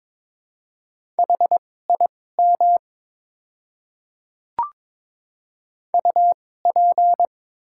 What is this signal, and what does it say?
Morse code, heard as a keyed tone, about 22 wpm, spaced as for 16 wpm, 700 Hz. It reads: HIM UP
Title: 2x Character Spacing - Top 100 Words - 22wpm